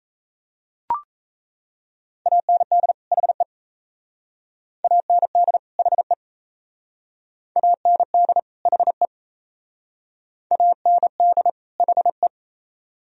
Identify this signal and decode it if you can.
Morse code, heard as a keyed tone, about 35 wpm, 700 Hz. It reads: AND HE AND HE AND HE AND HE
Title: Speed-Racing - Sets of 2 Words 28wpm